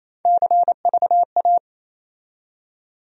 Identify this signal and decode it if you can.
Morse code, heard as a keyed tone, about 28 wpm, 700 Hz. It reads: CVA
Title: USA Call Sign Postfixes - 3 Letters 28wpm